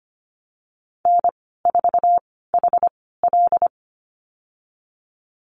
Morse code, heard as a keyed tone, about 25 wpm, spaced as for 16 wpm, 700 Hz. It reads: N4HL
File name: Lesson 15g - Number 5 - Call Signs 25wpm